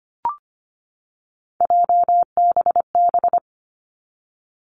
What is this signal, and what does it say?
Morse code, heard as a keyed tone, about 25 wpm, 700 Hz. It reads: JBB
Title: Three Challenging Letter-Number Combinations 25wpm